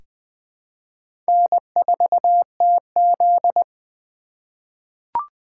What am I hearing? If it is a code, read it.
Morse code, heard as a keyed tone, about 20 wpm, 700 Hz. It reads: N4TZ
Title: K1USN SST QSO Practice - 20wpm